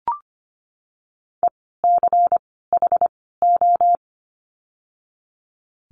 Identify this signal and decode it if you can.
Morse code, heard as a keyed tone, about 25 wpm, 700 Hz. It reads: E C H O
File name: course / 25:10wpm / 094 Lesson 16f - Letter c - Words 25wpm